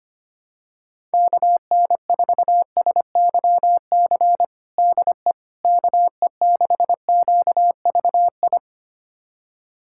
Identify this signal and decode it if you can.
Morse code, heard as a keyed tone, about 25 wpm, 700 Hz. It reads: KN4SYC DE KE6QVI
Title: USA Callsign DE Callsign - 6 Characters - 25wpm